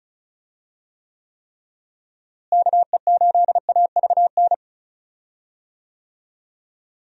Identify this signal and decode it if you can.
Morse code, heard as a keyed tone, about 34 wpm, 700 Hz. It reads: KE8AVN
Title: USA Call Signs - 6 Characters - Repeated 3x - 35wpm